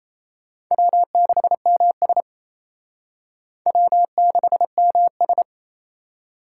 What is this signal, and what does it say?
Morse code, heard as a keyed tone, about 34 wpm, 700 Hz. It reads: W6MS W6MS
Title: Speed Racing - USA Call Signs 28wpm